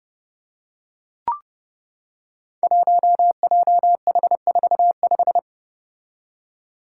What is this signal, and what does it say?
Morse code, heard as a keyed tone, about 30 wpm, 700 Hz. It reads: 1JH45
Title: Five Challenging Letter-Number Combinations 30wpm